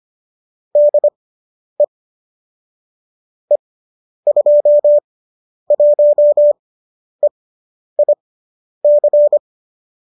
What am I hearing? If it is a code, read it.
Morse code, heard as a keyed tone, about 25 wpm, spaced as for 10 wpm, 570 Hz. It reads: DE E21EIC